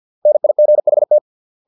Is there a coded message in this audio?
Yes: Morse code from 0.3 to 1.2 s, heard as a keyed tone, about 50 wpm, 600 Hz. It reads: NIGHT